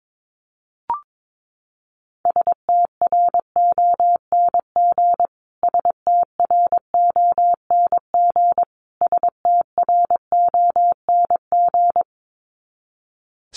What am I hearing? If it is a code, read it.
Morse code, heard as a keyed tone, about 22 wpm, 700 Hz. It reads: STRONG STRONG STRONG
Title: Top 500 Words Repeated 22wpm